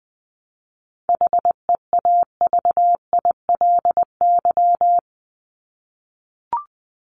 Morse code, heard as a keyed tone, about 20 wpm, 700 Hz. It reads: HEAVILY
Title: LY Suffix - Top 100 Words with Base Words 20wpm